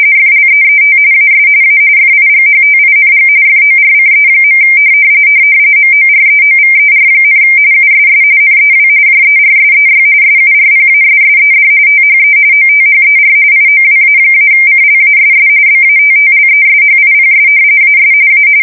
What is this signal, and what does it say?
RTTY, about 45.45 baud. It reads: FLEET AS A FIGHTING FORCE. MOREOVER, USS NEW JERSEY SURPASSED THE OTHER IOWA-CLASS BATTLESHIPS AS A
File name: fsk170 / battleshipnj